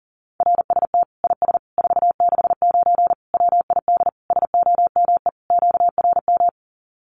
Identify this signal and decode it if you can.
Morse code, heard as a keyed tone, about 40 wpm, 700 Hz. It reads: RST IS 469 WID SOME QRM
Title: Common QSO Phrases 40wpm